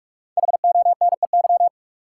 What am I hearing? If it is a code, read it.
Morse code, heard as a keyed tone, about 46 wpm, 700 Hz. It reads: HONEY
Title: Five Letter Words - 45wpm